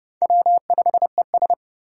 Morse code, heard as a keyed tone, about 36 wpm, 700 Hz. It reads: W5ES
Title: Speed Racing - USA 1x2 Call Signs 30wpm